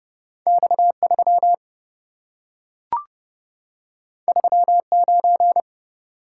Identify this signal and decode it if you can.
Morse code, heard as a keyed tone, about 27 wpm, 700 Hz. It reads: X3 39